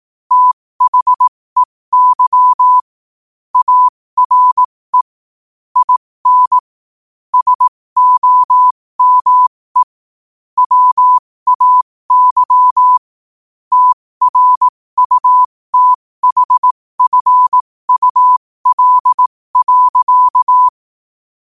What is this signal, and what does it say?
Morse code, heard as a keyed tone, about 18 wpm, spaced as for 15 wpm, 1000 Hz. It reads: THEY ARE IN SOME WAY TRUTHFUL.
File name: qotdfiles / 15wpm